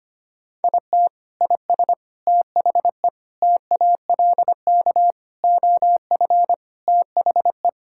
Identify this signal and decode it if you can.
Morse code, heard as a keyed tone, about 25 wpm, 700 Hz. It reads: IT IS THE TALK OF THE
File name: practice / 25wpm / Sentences with 4-Word Phrases - 25wpm